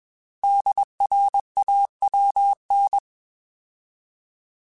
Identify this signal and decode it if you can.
Morse code, heard as a keyed tone, about 21 wpm, 800 Hz. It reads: DRAWN
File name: Alphabetical / Alphabetical mp3s / Ds @20